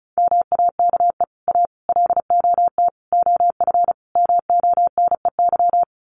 Morse code, heard as a keyed tone, about 34 wpm, 700 Hz. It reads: MAKE A LOT OF MONEY
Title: Sets of 5 Words Encore2 35wpm